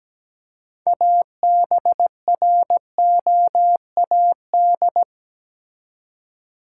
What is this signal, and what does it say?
Morse code, heard as a keyed tone, about 17 wpm, 700 Hz. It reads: ABROAD
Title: Six Letter Words - 17wpm